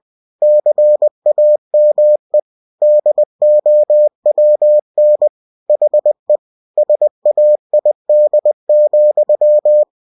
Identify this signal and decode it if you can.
Morse code, heard as a keyed tone, about 20 wpm, 600 Hz. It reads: CAME DOWN HE SAID,
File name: cw-books / winnie-the-pooh / winnie-20wpm-page-0070